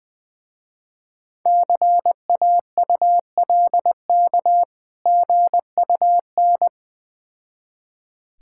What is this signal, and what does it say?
Morse code, heard as a keyed tone, about 20 wpm, 690 Hz. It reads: CAULK GUN
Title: Difficult Words with Hints - 20wpm